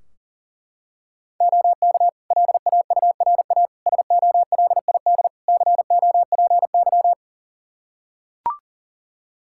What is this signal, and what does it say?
Morse code, heard as a keyed tone, about 40 wpm, 700 Hz. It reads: OK LAURA SOLID COPY
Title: K1USN SST QSO Practice 40wpm